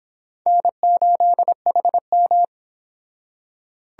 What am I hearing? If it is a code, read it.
Morse code, heard as a keyed tone, about 26 wpm, 700 Hz. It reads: N8HM